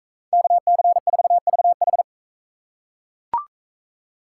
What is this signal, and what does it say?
Morse code, heard as a keyed tone, about 39 wpm, 700 Hz. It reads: KC4VH